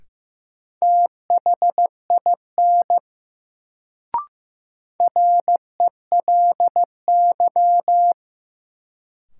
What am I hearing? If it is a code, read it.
Morse code, heard as a keyed tone, about 15 wpm, 700 Hz. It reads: THIN RELY
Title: Sentences from Top 1200 Words - New Word Drill - 15wpm